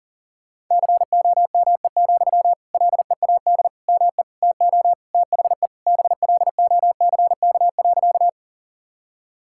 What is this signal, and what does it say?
Morse code, heard as a keyed tone, about 40 wpm, 700 Hz. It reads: COME, LEAD ME TO THE BLOCK.
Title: Sentences from Top 1600 Words 40wpm